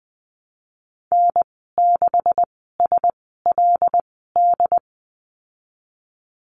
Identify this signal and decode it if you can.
Morse code, heard as a keyed tone, about 20 wpm, spaced as for 15 wpm, 700 Hz. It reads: N6SLD
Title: 167 Lesson 27g - Number 7 - Call Signs 20wpm